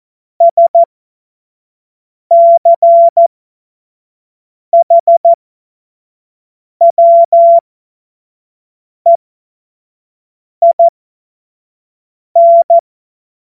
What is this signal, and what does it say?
Morse code, heard as a keyed tone, about 14 wpm, spaced as for 5 wpm, 680 Hz. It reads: SCHWEIN
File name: Witz-19-August-2025-25-F70